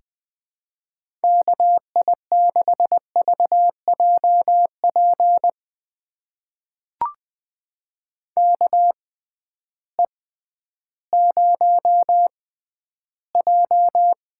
Morse code, heard as a keyed tone, about 20 wpm, spaced as for 12 wpm, 700 Hz. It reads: KI6VJP KE0J
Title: USA Call Signs - 6 Characters - 6x Character Spacing - 20wpm